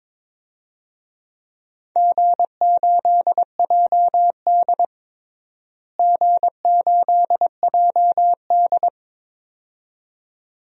Morse code, heard as a keyed tone, about 22 wpm, 700 Hz. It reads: G8JD G8JD
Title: DXCC-100 Call Signs - 4 Characters - Repeated 2x - 22wpm